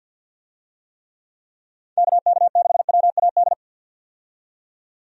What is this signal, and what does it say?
Morse code, heard as a keyed tone, about 53 wpm, 700 Hz. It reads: KK6WAD
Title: USA Call Signs - 6 Characters - 50wpm